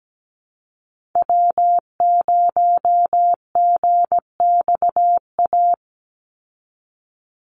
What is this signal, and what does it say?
Morse code, heard as a keyed tone, about 17 wpm, 700 Hz. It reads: W0GXA
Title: Call Signs Repeated 2x - 17wpm